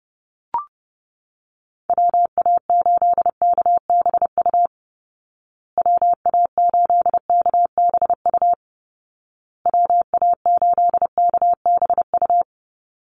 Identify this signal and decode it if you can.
Morse code, heard as a keyed tone, about 30 wpm, 700 Hz. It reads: WA8KBU WA8KBU WA8KBU
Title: Call Signs Repeated 2x - 30wpm